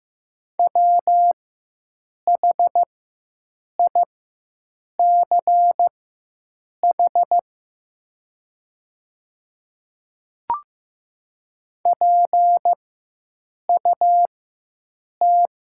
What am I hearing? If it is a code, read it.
Morse code, heard as a keyed tone, about 15 wpm, spaced as for 7 wpm, 700 Hz. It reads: WHICH PUT
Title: 4x Character Spacing - Top 300 Words - 15wpm